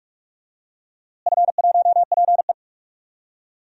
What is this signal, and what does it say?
Morse code, heard as a keyed tone, about 41 wpm, 700 Hz. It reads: F1PE
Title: DXCC-100 Call Signs - 4 Characters - Repeated 3x - 45wpm